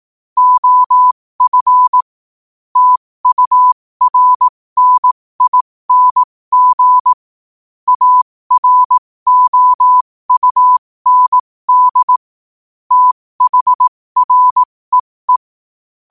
Morse code, heard as a keyed tone, about 18 wpm, spaced as for 15 wpm, 1000 Hz. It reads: OF TURNING AROUND THREE
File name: qotdfiles / 15wpm